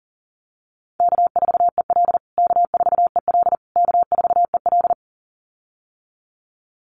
Morse code, heard as a keyed tone, about 40 wpm, 700 Hz. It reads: K4EL K4EL K4EL
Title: USA Call Signs Repeated 3x 40wpm